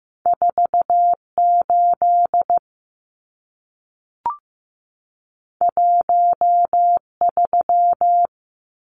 Morse code, heard as a keyed tone, about 15 wpm, 700 Hz. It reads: 48 13
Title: Numbers - 2-Digit 15wpm